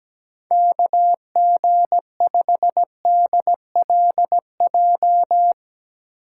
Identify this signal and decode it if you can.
Morse code, heard as a keyed tone, about 17 wpm, 700 Hz. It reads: KG5DLJ
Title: USA Call Signs - 8x Character Spacing - 17wpm